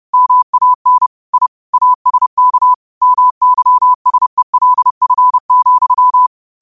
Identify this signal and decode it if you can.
Morse code, heard as a keyed tone, about 30 wpm, 1000 Hz. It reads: MAN I ASK MYSELF,
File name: qotdfiles / 30wpm